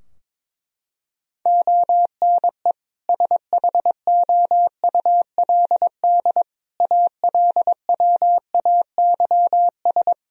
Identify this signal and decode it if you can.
Morse code, heard as a keyed tone, about 22 wpm, 700 Hz. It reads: ONE SHOULD ALWAYS
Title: Sentences from Top 1000 Words - Review Part 3 - 22wpm